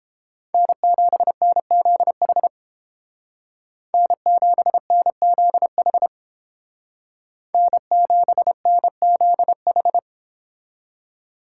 Morse code, heard as a keyed tone, about 31 wpm, 690 Hz. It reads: N7NZH N7NZH N7NZH